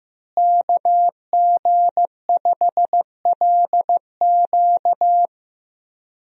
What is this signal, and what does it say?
Morse code, heard as a keyed tone, about 15 wpm, 700 Hz. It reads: KG5LQ
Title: USA Call Signs - 5 Characters - Repeated 3x - 15wpm